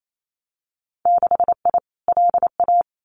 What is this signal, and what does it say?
Morse code, heard as a keyed tone, about 28 wpm, 700 Hz. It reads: 6I LA
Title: Winter Field Day Exchanges Encore 28wpm